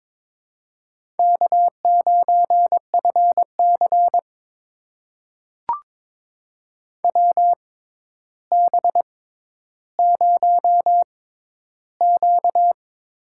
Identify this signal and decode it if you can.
Morse code, heard as a keyed tone, about 23 wpm, spaced as for 11 wpm, 700 Hz. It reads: K9FC WB0Q
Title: USA Call Signs - 4 Characters - 6x Character Spacing - 22wpm